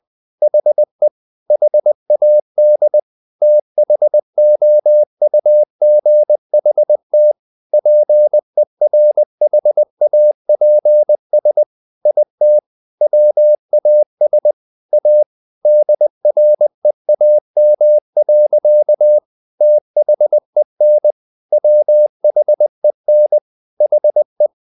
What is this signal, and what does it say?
Morse code, heard as a keyed tone, about 20 wpm, 600 Hz. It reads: HE HAD THOUGHT PERHAPS IT WAS A DREAM. THEN WHEN HE
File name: the old man and the sea part 2 20wpm